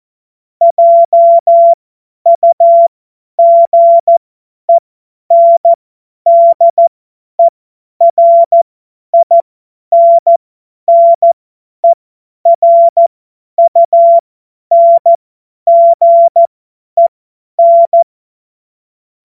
Morse code, heard as a keyed tone, about 14 wpm, spaced as for 10 wpm, 680 Hz. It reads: JUGENDERINNERUNGEN
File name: Morsekurs / Witze / Witz 30. August 2022 50 F70